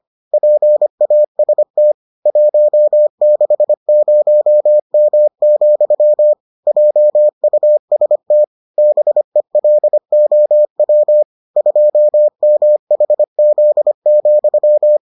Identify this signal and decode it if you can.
Morse code, heard as a keyed tone, about 25 wpm, 600 Hz. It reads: PAST 160M, JUST BELOW 2MHZ,